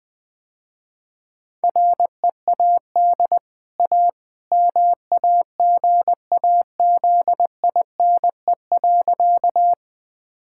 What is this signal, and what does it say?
Morse code, heard as a keyed tone, about 20 wpm, 700 Hz. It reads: READ A MAGAZINE.
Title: Easy Sentences with Hints - 20wpm